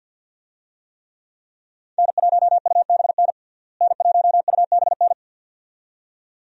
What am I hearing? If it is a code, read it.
Morse code, heard as a keyed tone, about 50 wpm, 700 Hz. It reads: N1UBN N1UBN
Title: USA Call Signs - 5 Characters - Repeated 2x - 50wpm